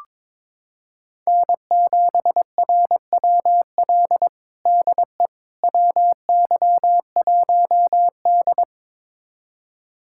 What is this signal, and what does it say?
Morse code, heard as a keyed tone, about 21 wpm, 700 Hz. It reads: N7RWL DE WY1D
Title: USA Callsign DE Callsign 22wpm